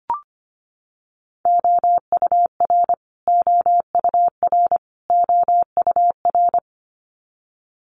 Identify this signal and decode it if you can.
Morse code, heard as a keyed tone, about 25 wpm, 700 Hz. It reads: OUR OUR OUR
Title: Top 500 Words Repeated 25wpm